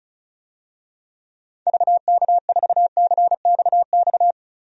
Morse code, heard as a keyed tone, about 36 wpm, 700 Hz. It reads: VK4CXX
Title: DXCC-100 Call Signs - 6 Characters - 35wpm